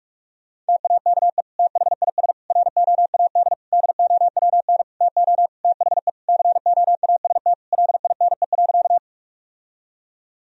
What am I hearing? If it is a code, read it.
Morse code, heard as a keyed tone, about 46 wpm, 700 Hz. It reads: TAKE THIS ROAD DOWN TO THE COAST LINE.
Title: Sentences from Top 1000 Words - Review Part 4 - 45wpm